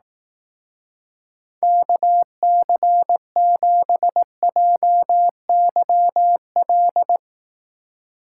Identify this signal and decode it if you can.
Morse code, heard as a keyed tone, about 18 wpm, 700 Hz. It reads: KC7JYL
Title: Speed Racing - USA Call Signs 15wpm